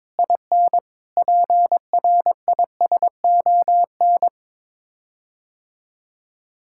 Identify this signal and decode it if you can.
Morse code, heard as a keyed tone, about 22 wpm, 700 Hz. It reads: IN PRISON
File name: practice / 22wpm / Sets of 6 Words - 22wpm